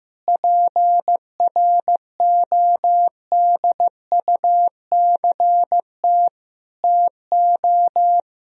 Morse code, heard as a keyed tone, about 15 wpm, 700 Hz. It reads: PRODUCT TO
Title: Sentences from Top 1000 Words - Review Part 3 - 15wpm